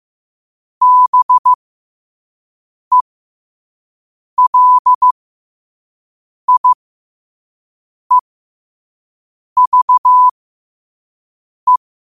Morse code, heard as a keyed tone, about 15 wpm, spaced as for 5 wpm, 1000 Hz. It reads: BELIEVE
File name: qotdfiles / 5wpm